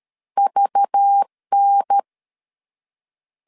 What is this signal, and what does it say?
Morse code, heard as a keyed tone, about 13 wpm, 800 Hz. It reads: VN
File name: CW / DET / 1 6 1500C S A 12wpm 3440